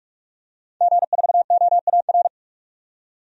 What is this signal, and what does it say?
Morse code, heard as a keyed tone, about 46 wpm, 700 Hz. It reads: G4OAR
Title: DXCC-100 Call Signs - 5 Characters - 45wpm